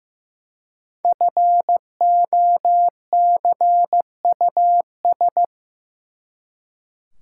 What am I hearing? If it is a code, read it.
Morse code, heard as a keyed tone, about 15 wpm, 700 Hz. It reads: FOCUS
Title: Five Letter Words - 15wpm